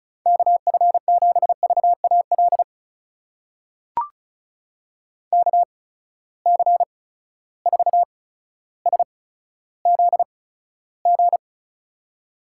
Morse code, heard as a keyed tone, about 36 wpm, spaced as for 15 wpm, 700 Hz. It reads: KF7VAL KC4SZG
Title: USA Call Signs - 6 Characters - 8x Character Spacing - 35wpm